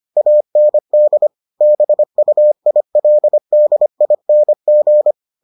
Morse code, heard as a keyed tone, about 25 wpm, 600 Hz. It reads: AND BUILDING